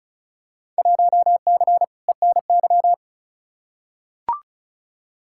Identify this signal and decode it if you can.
Morse code, heard as a keyed tone, about 36 wpm, 700 Hz. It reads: 1C ENY
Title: Field Day Exchanges Encore - 35wpm